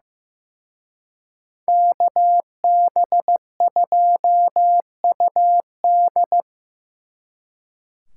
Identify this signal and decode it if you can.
Morse code, heard as a keyed tone, about 18 wpm, 700 Hz. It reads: KB2UD
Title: Speed Racing - USA Call Signs 15wpm